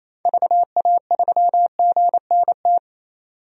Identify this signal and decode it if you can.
Morse code, heard as a keyed tone, about 28 wpm, 700 Hz. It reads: VA3GNT
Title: DXCC-100 Call Signs - 6 Characters - 28wpm